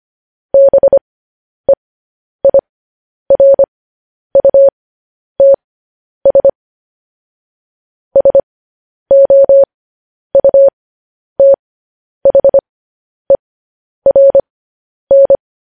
Morse code, heard as a keyed tone, about 25 wpm, spaced as for 10 wpm, 550 Hz. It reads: BEIRUTS SOUTHERN
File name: podcast / News headlines-1x-25x10